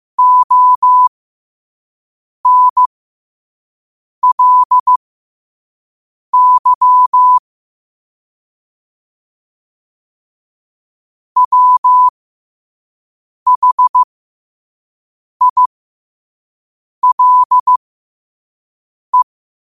Morse code, heard as a keyed tone, about 15 wpm, spaced as for 5 wpm, 1000 Hz. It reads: ONLY WHILE
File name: qotdfiles / 5wpm